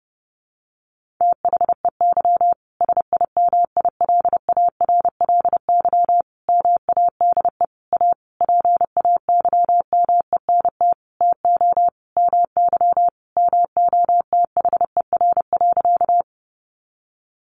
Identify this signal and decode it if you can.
Morse code, heard as a keyed tone, about 30 wpm, 700 Hz. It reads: THEY SIMILARLY MADE A PAYMENT TO MY MOTHER.
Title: Sentences from Top 1400 Words 30wpm